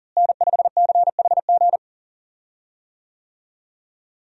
Morse code, heard as a keyed tone, about 40 wpm, 700 Hz. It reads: N5CHG
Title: USA Call Signs - 5 Characters - Repeated 2x - 40wpm